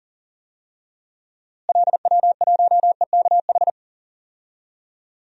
Morse code, heard as a keyed tone, about 40 wpm, 700 Hz. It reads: LW1EKH
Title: DXCC-100 Call Signs - 6 Characters - Repeated 2x - 40wpm